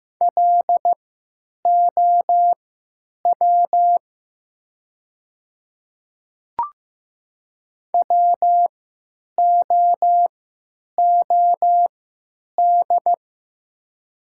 Morse code, heard as a keyed tone, about 15 wpm, spaced as for 9 wpm, 700 Hz. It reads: LOW WOOD
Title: 3x Character Spacing - Top 300 Words - 15wpm